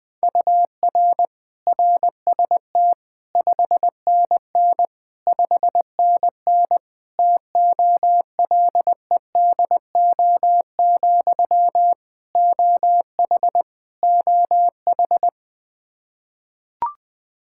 Morse code, heard as a keyed tone, about 20 wpm, 700 Hz. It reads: UR RST 5NN 5NN TOLEDO, OH OH
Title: Practice QSO - Exchange 1 - Beginning - 20wpm